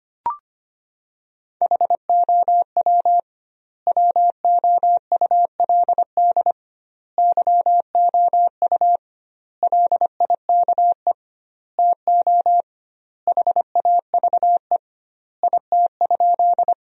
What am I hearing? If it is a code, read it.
Morse code, heard as a keyed tone, about 25 wpm, 700 Hz. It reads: HOW WOULD YOU LIKE TO HAVE IT?
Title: 2x Word-Spacing - Sentences from Top 100 Words - 25wpm